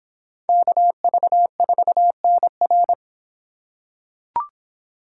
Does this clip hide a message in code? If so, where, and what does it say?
Yes: Morse code from 0.5 to 2.9 s, heard as a keyed tone, about 26 wpm, 700 Hz. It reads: KV4NR